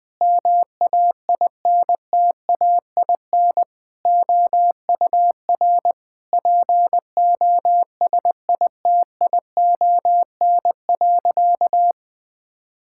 Morse code, heard as a keyed tone, about 20 wpm, 700 Hz. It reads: MAINTAIN OUR POSITION.
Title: Sentences from Top 1000 Words - Review Part 5 - 20wpm